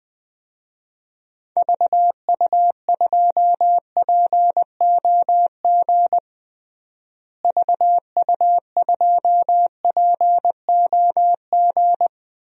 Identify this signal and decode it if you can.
Morse code, heard as a keyed tone, about 20 wpm, 700 Hz. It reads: VU2POG VU2POG
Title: DXCC-100 Call Signs - 6 Characters - Repeated 3x - 20wpm